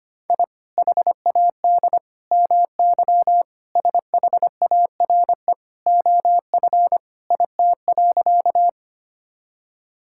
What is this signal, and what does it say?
Morse code, heard as a keyed tone, about 25 wpm, 700 Hz. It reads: I HAD MY SHARE OF IT.
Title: Sentences from Top 1000 Words - Review Part 6 - 25wpm